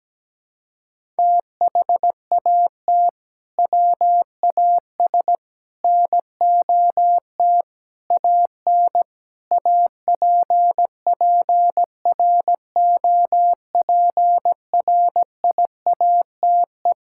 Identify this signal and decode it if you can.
Morse code, heard as a keyed tone, about 17 wpm, 700 Hz. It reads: THAT WAS NOT AN APPROPRIATE
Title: Sentences from Top 1000 Words - Review Part 5 - 17wpm